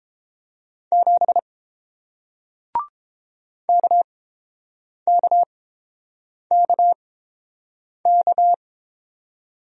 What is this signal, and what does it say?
Morse code, heard as a keyed tone, about 31 wpm, spaced as for 7 wpm, 700 Hz. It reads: 7 KKKK